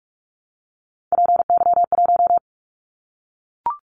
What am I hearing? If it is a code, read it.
Morse code, heard as a keyed tone, about 44 wpm, 700 Hz. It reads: PY1
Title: Callsign Prefixes for Top 100 DXCC - 45wpm